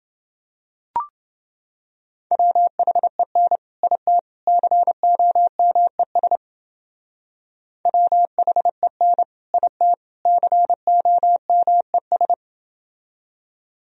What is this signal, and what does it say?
Morse code, heard as a keyed tone, about 29 wpm, 700 Hz. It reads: WHEN IT COMES WHEN IT COMES